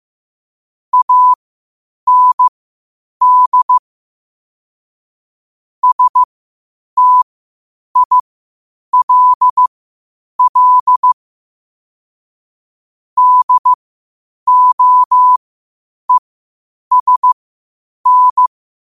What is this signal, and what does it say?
Morse code, heard as a keyed tone, about 15 wpm, spaced as for 8 wpm, 1000 Hz. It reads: AND STILL DOESN
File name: qotdfiles / 8wpm